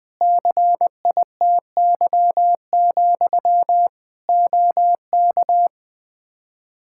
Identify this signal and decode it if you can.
Morse code, heard as a keyed tone, about 20 wpm, 700 Hz. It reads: CITY, OK
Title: Practice QSO - Exchange 1 - Beginning - 20wpm